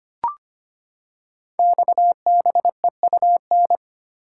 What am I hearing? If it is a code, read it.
Morse code, heard as a keyed tone, about 25 wpm, 700 Hz. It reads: XBEUN
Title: Code Groups - 5 Letters - 25wpm